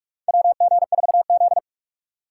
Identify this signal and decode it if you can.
Morse code, heard as a keyed tone, about 44 wpm, 700 Hz. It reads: WG4Z
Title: USA Call Signs - 4 Characters - 3x Character Spacing - 45wpm